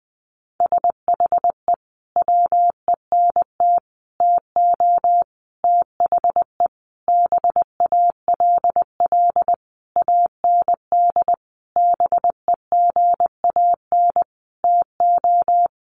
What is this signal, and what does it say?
Morse code, heard as a keyed tone, about 19 wpm, 700 Hz. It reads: SHE WENT TO THE BALL AND BEGAN TO
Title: Sentences from Top 900 Words 20wpm